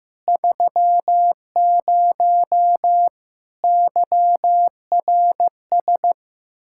Morse code, heard as a keyed tone, about 16 wpm, 700 Hz. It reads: 30 YRS
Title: K1USN SST QSO Practice 15wpm